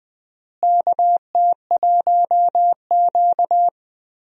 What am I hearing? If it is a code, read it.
Morse code, heard as a keyed tone, about 20 wpm, 700 Hz. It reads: KT1Q
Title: USA Call Signs - 4 Characters - Repeated 3x - 20wpm